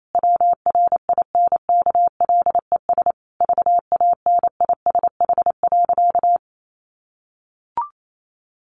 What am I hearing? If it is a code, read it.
Morse code, heard as a keyed tone, about 28 wpm, 700 Hz. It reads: WRINKLES VANISH.
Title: A Sound Of Thunder 28wpm